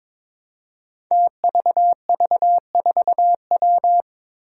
Morse code, heard as a keyed tone, about 23 wpm, 700 Hz. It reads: TVV4W
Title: Five Challenging Letter-Number Combinations 22wpm